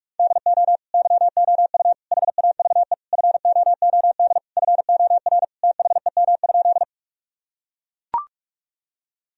Morse code, heard as a keyed tone, about 41 wpm, 700 Hz. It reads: DO YOU HAVE FOOD FOR THEM?
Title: Sentences from Top 1000 Words - Review Part 1 - 45wpm